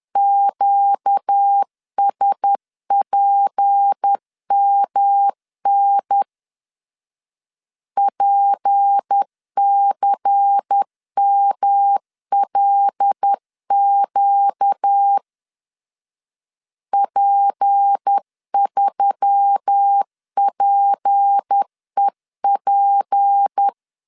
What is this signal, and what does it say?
Morse code, heard as a keyed tone, about 10 wpm, 800 Hz. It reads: QSPMN PCMLQ P3PEP
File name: LEZIONI / 6 PKX0 / 6 P 10wpm 2432